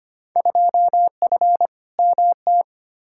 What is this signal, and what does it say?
Morse code, heard as a keyed tone, about 25 wpm, 700 Hz. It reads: 2F MT
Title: Field Day Exchanges Encore - 25wpm